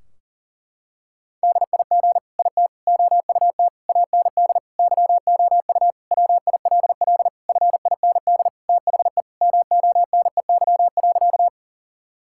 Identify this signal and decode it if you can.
Morse code, heard as a keyed tone, about 40 wpm, 700 Hz. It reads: DIG IT OUT AND YOU WILL FIND THE MONEY.
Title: Sentences from Top 1600 Words 40wpm